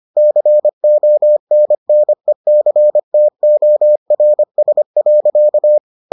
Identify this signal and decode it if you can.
Morse code, heard as a keyed tone, about 25 wpm, 600 Hz. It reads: CONNECTORS.